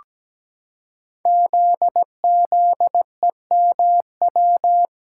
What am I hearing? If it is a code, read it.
Morse code, heard as a keyed tone, about 17 wpm, 700 Hz. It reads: ZZEMW
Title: Code Groups - 5 Letters - 17wpm